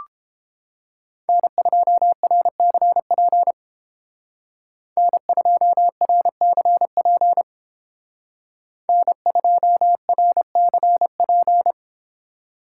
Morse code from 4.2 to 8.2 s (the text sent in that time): N2RCP